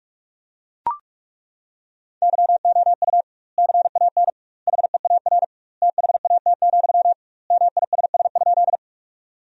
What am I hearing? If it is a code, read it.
Morse code, heard as a keyed tone, about 42 wpm, 700 Hz. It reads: YOU CAN HEAR THAT, MISS?
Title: 2x Word-Spacing - Sentences from Top 500 Words - 45wpm